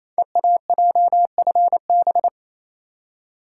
Morse code, heard as a keyed tone, about 28 wpm, 700 Hz. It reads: EAJFB
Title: Code Groups - 5 Letter:Numbers - 28wpm